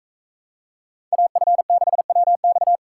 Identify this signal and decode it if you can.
Morse code, heard as a keyed tone, about 41 wpm, 700 Hz. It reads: AF6WX